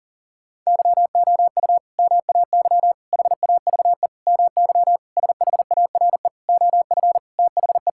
Morse code, heard as a keyed tone, about 40 wpm, 700 Hz. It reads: YOU MAY HAVE MY SHARE OF THE
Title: Sentences from Top 1000 Words - Review Part 6 - 40wpm